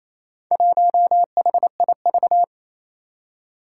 Morse code, heard as a keyed tone, about 28 wpm, 700 Hz. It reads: 1HIV
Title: Four Challenging Letter-Number Combinations 28wpm